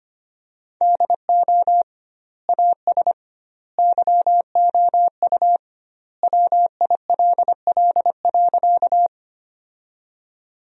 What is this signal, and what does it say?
Morse code, heard as a keyed tone, about 25 wpm, 700 Hz. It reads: DO AS YOU WILL.
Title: 2x Word-Spacing - Sentences from Top 100 Words - 25wpm